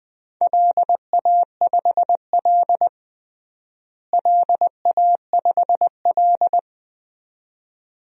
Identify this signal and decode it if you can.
Morse code, heard as a keyed tone, about 20 wpm, 700 Hz. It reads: LA5L LA5L
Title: DXCC-100 Call Signs - 4 Characters - Repeated 2x - 20wpm